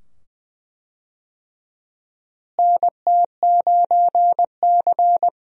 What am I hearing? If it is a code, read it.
Morse code, heard as a keyed tone, about 20 wpm, 700 Hz. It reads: NT9C
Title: USA Call Signs - 4 Characters - Repeated 2x - 20wpm